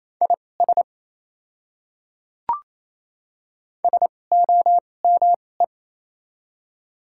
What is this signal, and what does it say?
Morse code, heard as a keyed tone, about 27 wpm, spaced as for 20 wpm, 690 Hz. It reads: IS SOME
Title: 2x Character Spacing - Top 100 Words - 28wpm